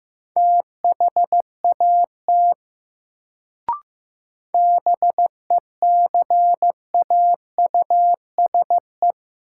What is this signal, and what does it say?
Morse code, heard as a keyed tone, about 15 wpm, 700 Hz. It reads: THAT BECAUSE
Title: Sets of 2 Words - New Word Drill - 15wpm